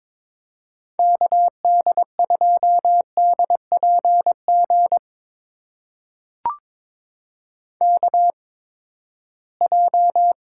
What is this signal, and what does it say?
Morse code, heard as a keyed tone, about 22 wpm, spaced as for 15 wpm, 700 Hz. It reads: KD2DPG KJ
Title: USA Call Signs - 6 Characters - 8x Character Spacing - 22wpm